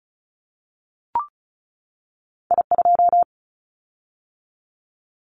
Morse code, heard as a keyed tone, about 36 wpm, 700 Hz. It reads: I2
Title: Callsign Prefixes for Top 25 DXCC - 35wpm